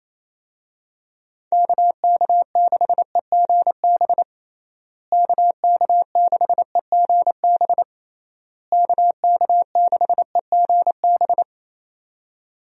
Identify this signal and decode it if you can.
Morse code, heard as a keyed tone, about 28 wpm, 700 Hz. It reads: KK6EGB KK6EGB KK6EGB
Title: USA Call Signs - 6 Characters - Repeated 3x - 28wpm